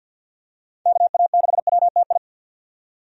Morse code, heard as a keyed tone, about 51 wpm, 700 Hz. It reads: KA6WTI
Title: USA Call Signs - 6 Characters - 50wpm